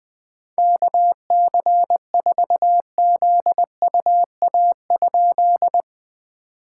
Morse code, heard as a keyed tone, about 21 wpm, 690 Hz. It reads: KC4ZUA?
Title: POTA QSO Practice - 20wpm